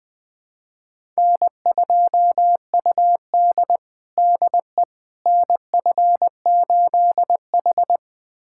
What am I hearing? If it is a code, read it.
Morse code, heard as a keyed tone, about 20 wpm, 700 Hz. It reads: N2UD DE NF8H
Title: USA Callsign DE Callsign - 4 Characters - 20wpm